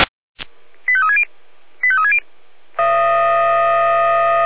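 Five-tone selective call (ZVEI), heard as quick runs of five tones, tones of about 70 ms. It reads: 85360 85360
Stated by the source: ZVEI five-tone selective call 85342 received on SDR.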